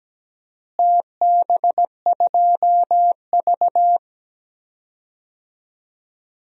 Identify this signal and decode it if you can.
Morse code, heard as a keyed tone, about 17 wpm, 700 Hz. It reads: TB2V
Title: DXCC-100 Call Signs - 4 Characters - Repeated 3x - 17wpm